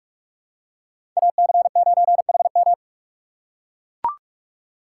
Morse code, heard as a keyed tone, about 44 wpm, 700 Hz. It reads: AC9HM
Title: USA Call Signs - 5 Characters - 2x Character Spacing - 45wpm